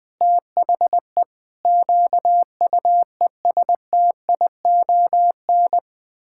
Morse code, heard as a keyed tone, about 21 wpm, 700 Hz. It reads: THE QUESTION
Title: Sentences from Top 1600 Words 20wpm